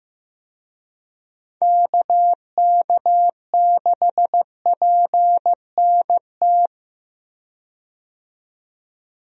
Morse code, heard as a keyed tone, about 15 wpm, 700 Hz. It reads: KK6PNT
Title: USA Call Signs - 6 Characters - 15wpm